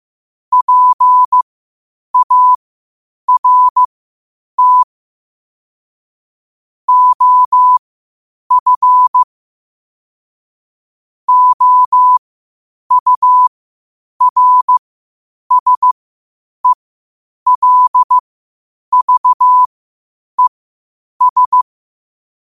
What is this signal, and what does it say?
Morse code, heard as a keyed tone, about 15 wpm, spaced as for 8 wpm, 1000 Hz. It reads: PART OF OURSELVES
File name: qotdfiles / 8wpm